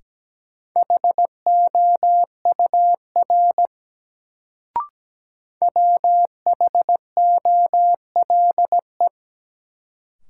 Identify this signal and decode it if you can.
Morse code, heard as a keyed tone, about 17 wpm, 700 Hz. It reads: HOUR WHOLE
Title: Sentences from Top 400 Words - New Word Drill - 17wpm